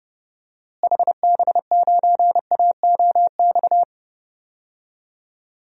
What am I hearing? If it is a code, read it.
Morse code, heard as a keyed tone, about 30 wpm, 700 Hz. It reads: HB9AOX
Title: DXCC-100 Call Signs - 6 Characters - Repeated 2x - 30wpm